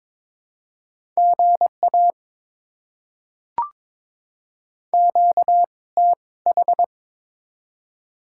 Morse code, heard as a keyed tone, about 22 wpm, spaced as for 18 wpm, 690 Hz. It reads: GA QTH
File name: Top 20 QSO Elements - 2x Character Spacing - 22wpm